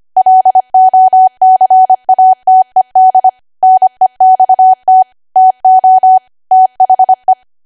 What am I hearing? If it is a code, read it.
Morse code, heard as a keyed tone, about 25 wpm, 750 Hz. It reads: LOCATED NEXT TO THE
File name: Archive / 25 WPM / 25WPM